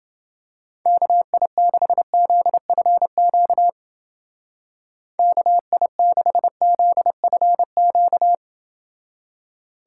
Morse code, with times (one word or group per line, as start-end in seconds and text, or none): 0.9-3.7 KI6ZFQ
5.2-8.3 KI6ZFQ